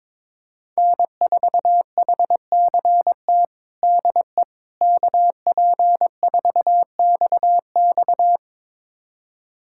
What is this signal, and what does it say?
Morse code, heard as a keyed tone, about 21 wpm, 700 Hz. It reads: N4HCT DE KP4XX
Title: USA Callsign DE Callsign - 5 Characters - 22wpm